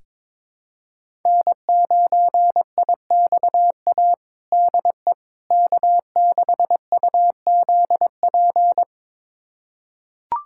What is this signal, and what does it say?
Morse code, heard as a keyed tone, about 22 wpm, 700 Hz. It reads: N9IXA DE K6UZP
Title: USA Callsign DE Callsign - 5 Characters - 22wpm